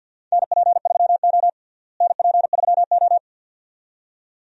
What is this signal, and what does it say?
Morse code, heard as a keyed tone, about 50 wpm, 700 Hz. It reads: NP3O NP3O
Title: USA Call Signs - 4 Characters - Repeated 2x - 50wpm